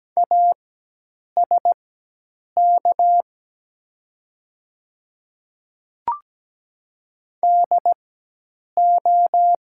Morse code, heard as a keyed tone, about 17 wpm, spaced as for 8 wpm, 700 Hz. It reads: ASK DO
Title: 4x Character Spacing - Top 200 Words - 17wpm